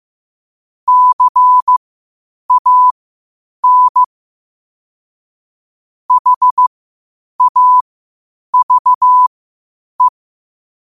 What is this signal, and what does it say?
Morse code, heard as a keyed tone, about 15 wpm, spaced as for 8 wpm, 1000 Hz. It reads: CAN HAVE
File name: qotdfiles / 8wpm